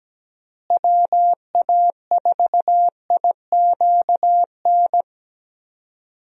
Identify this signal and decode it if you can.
Morse code, heard as a keyed tone, about 17 wpm, 700 Hz. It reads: WA4IQN
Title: USA Call Signs - 6 Characters - Repeated 3x - 17wpm